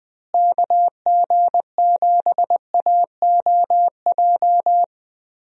Morse code, heard as a keyed tone, about 19 wpm, 700 Hz. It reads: KG7AOJ
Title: USA Call Signs - 6 Characters - Repeated 3x - 20wpm